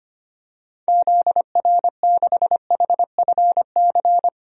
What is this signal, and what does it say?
Morse code, heard as a keyed tone, about 25 wpm, 700 Hz. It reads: ZR6HFC
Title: DXCC-100 Call Signs - 6 Characters - 25wpm